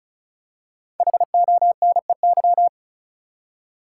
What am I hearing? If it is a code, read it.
Morse code, heard as a keyed tone, about 34 wpm, 700 Hz. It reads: HONEY
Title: Five Letter Words - 35wpm